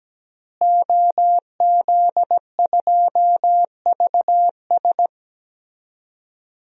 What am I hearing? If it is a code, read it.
Morse code, heard as a keyed tone, about 17 wpm, 700 Hz. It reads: OZ2VS
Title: DXCC-100 Call Signs - 5 Characters - Repeated 2x - 17wpm